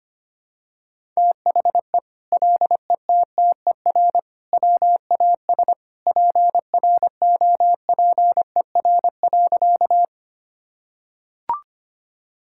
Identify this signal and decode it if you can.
Morse code, heard as a keyed tone, about 25 wpm, 700 Hz. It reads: THE LETTER WAS PROPER.
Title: Sentences from Top 1000 Words - Review Part 5 - 25wpm